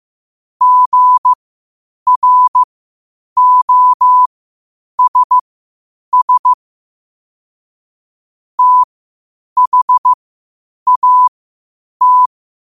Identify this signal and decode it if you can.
Morse code, heard as a keyed tone, about 15 wpm, spaced as for 8 wpm, 1000 Hz. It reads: GROSS THAT
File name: qotdfiles / 8wpm